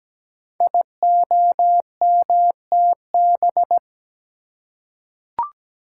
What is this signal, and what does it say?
Morse code, heard as a keyed tone, about 17 wpm, 700 Hz. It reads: IOMTB
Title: Code Groups - 5 Letters - 17wpm